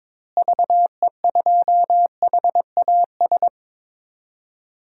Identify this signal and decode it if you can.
Morse code, heard as a keyed tone, about 22 wpm, 700 Hz. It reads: VE2HAS
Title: DXCC-100 Call Signs - 6 Characters - Repeated 3x - 22wpm